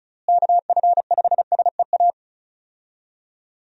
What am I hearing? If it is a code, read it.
Morse code, heard as a keyed tone, about 36 wpm, 700 Hz. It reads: KF5SEA
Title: USA Call Signs - 6 Characters - Repeated 3x - 35wpm